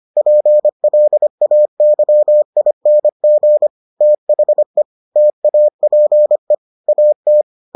Morse code, heard as a keyed tone, about 25 wpm, 600 Hz. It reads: PLAYING THE TAPE AT